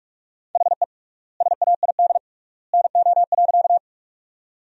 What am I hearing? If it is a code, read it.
Morse code, heard as a keyed tone, about 45 wpm, 700 Hz. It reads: HE SAID NO.
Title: 3x Word-Spacing - Sentences from Top 100 Words - 45wpm